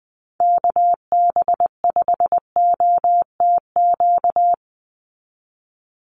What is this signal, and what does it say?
Morse code, heard as a keyed tone, about 20 wpm, 700 Hz. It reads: KB5OTQ
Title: USA Call Signs 20wpm